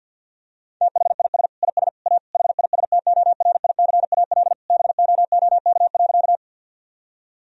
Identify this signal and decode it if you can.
Morse code, heard as a keyed tone, about 50 wpm, 700 Hz. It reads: THIS IS A HISTORICAL BOOK.
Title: Sentences from Top 1000 Words - Review Part 4 - 50wpm